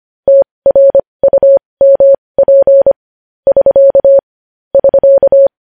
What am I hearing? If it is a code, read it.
Morse code, heard as a keyed tone, about 25 wpm, spaced as for 20 wpm, 550 Hz. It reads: TRUMP <SK> <SK>
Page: news headlines in morse code at 20 wpm